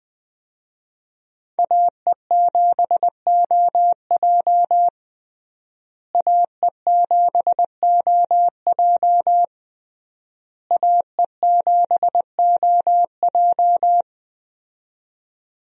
Morse code, heard as a keyed tone, about 20 wpm, 700 Hz. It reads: AE7OJ AE7OJ AE7OJ
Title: USA Call Signs - 5 Characters - Repeated 3x - 20wpm